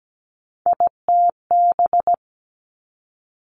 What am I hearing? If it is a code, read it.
Morse code, heard as a keyed tone, about 17 wpm, 700 Hz. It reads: ITB
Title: Three Challenging Letter-Number Combinations 17wpm